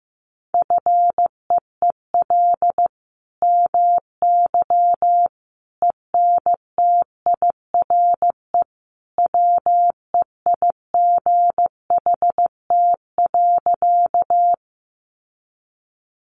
Morse code, heard as a keyed tone, about 14 wpm, 700 Hz. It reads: FEEL MY ENTIRE WEIGHT.
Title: Sentences from Top 800 Words 15wpm